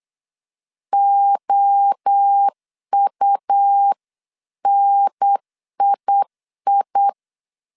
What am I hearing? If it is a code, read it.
Morse code, heard as a keyed tone, about 8 wpm, 800 Hz. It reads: OUNII